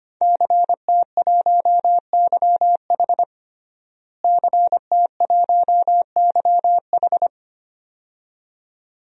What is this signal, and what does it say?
Morse code, heard as a keyed tone, about 25 wpm, 700 Hz. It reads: CT1YH CT1YH
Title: DXCC-100 Call Signs - 5 Characters - Repeated 2x - 25wpm